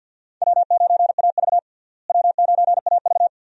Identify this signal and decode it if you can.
Morse code, heard as a keyed tone, about 52 wpm, 700 Hz. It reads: W9AV W9AV
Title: USA Call Signs - 4 Characters - Repeated 2x - 50wpm